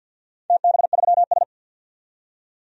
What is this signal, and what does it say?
Morse code, heard as a keyed tone, about 47 wpm, 700 Hz. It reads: TB3S
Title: DXCC-100 Call Signs - 4 Characters - 50wpm